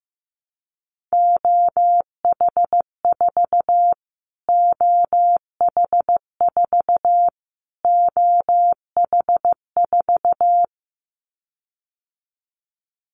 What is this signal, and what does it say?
Morse code, heard as a keyed tone, about 14 wpm, 700 Hz. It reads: OH4 OH4 OH4
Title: Callsign Prefixes for Top 25 DXCC - Repeated 3x - 15wpm